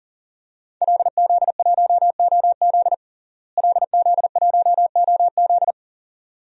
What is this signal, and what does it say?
Morse code, heard as a keyed tone, about 40 wpm, 700 Hz. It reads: LZ1OZ LZ1OZ
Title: DXCC-100 Call Signs - 5 Characters - Repeated 2x - 40wpm